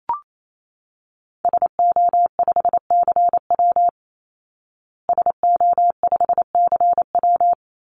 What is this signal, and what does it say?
Morse code, heard as a keyed tone, about 28 wpm, 700 Hz. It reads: SO5CW SO5CW
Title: Call Signs Repeated 3x - 28wpm